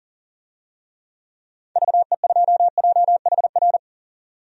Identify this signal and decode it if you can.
Morse code, heard as a keyed tone, about 40 wpm, 700 Hz. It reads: VE2JHR
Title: DXCC-100 Call Signs - 6 Characters - 40wpm